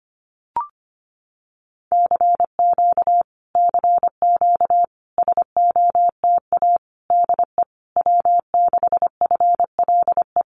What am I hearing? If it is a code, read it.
Morse code, heard as a keyed tone, about 25 wpm, 700 Hz. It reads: CQ CQ SOTA DE W6FLE
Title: Common QSO Phrases Encore 25wpm